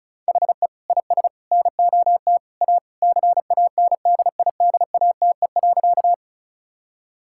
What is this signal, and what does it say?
Morse code, heard as a keyed tone, about 36 wpm, 700 Hz. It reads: HE IS NOT A CANDIDATE.
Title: Sentences from Top 1600 Words 35wpm